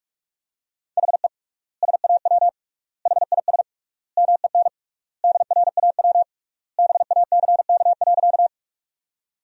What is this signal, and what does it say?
Morse code, heard as a keyed tone, about 45 wpm, 700 Hz. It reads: HE SAW HIS MEN DRAW BACK.
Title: 3x Word-Spacing - Sentences from Top 300 Words - 45wpm